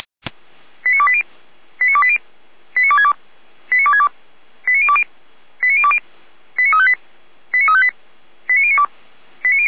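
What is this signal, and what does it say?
Five-tone selective call (ZVEI), heard as quick runs of five tones, tones of about 70 ms. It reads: 89280 89280 89262 89262 89020 89020 89358 89358 89092
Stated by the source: ZVEI five-tone fire brigade alert received by SDR, Feuerwehr Orlamünde.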